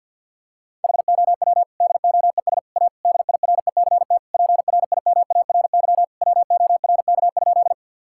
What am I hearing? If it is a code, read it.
Morse code, heard as a keyed tone, about 50 wpm, 700 Hz. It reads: HOW DOES A DIRECT PRIMARY WORK?
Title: Sentences from Top 1000 Words - Review Part 5 - 50wpm